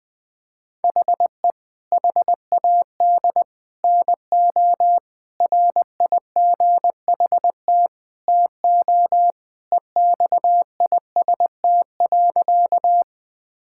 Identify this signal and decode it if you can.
Morse code, heard as a keyed tone, about 20 wpm, 700 Hz. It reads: HE HAD NO RIGHT TO EXIST.
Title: Sentences from Top 1000 Words - Review Part 5 - 20wpm